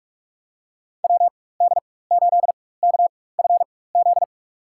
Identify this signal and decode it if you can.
Morse code, heard as a keyed tone, about 46 wpm, spaced as for 21 wpm, 700 Hz. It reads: WD8KFZ
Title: USA Call Signs - 6 Characters - 4x Character Spacing - 45wpm